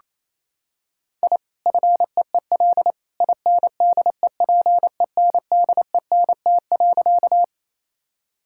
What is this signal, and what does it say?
Morse code, heard as a keyed tone, about 27 wpm, 700 Hz. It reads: I FEEL INDEPENDENT.
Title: Sentences from Top 1000 Words - Review Part 6 - 28wpm